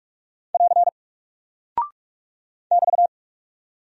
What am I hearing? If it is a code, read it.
Morse code, heard as a keyed tone, about 44 wpm, 700 Hz. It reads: <AR> <BT>
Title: Single Character (Letter:Number:Punctuation:Prosign) Alt 45wpm